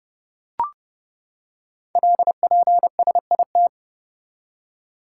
Morse code, heard as a keyed tone, about 30 wpm, 700 Hz. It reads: LPSIT
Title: Code Groups - 5 Letters - 30wpm